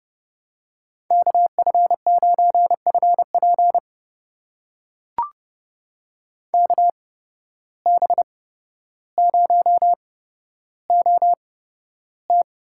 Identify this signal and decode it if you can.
Morse code, heard as a keyed tone, about 30 wpm, spaced as for 13 wpm, 700 Hz. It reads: KF9FP KB0OT
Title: USA Call Signs - 8x Character Spacing - 30wpm